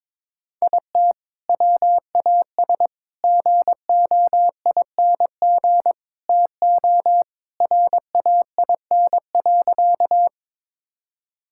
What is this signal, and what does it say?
Morse code, heard as a keyed tone, about 22 wpm, 700 Hz. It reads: IT WAS GOING TO RAIN.
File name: practice / 22wpm / Sentences with 4-Word Phrases - 22wpm